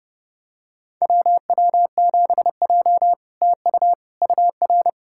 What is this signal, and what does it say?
Morse code, heard as a keyed tone, about 30 wpm, 700 Hz. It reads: WW7J TU UR
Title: SOTA QSO Practice 30wpm